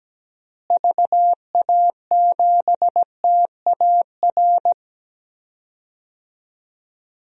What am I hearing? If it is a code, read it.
Morse code, heard as a keyed tone, about 17 wpm, 700 Hz. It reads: VA7TAR
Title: DXCC-100 Call Signs - 6 Characters - Repeated 3x - 17wpm